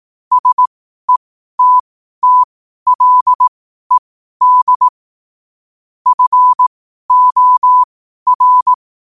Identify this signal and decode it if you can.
Morse code, heard as a keyed tone, about 18 wpm, spaced as for 12 wpm, 1000 Hz. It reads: SETTLED FOR
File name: qotdfiles / 12wpm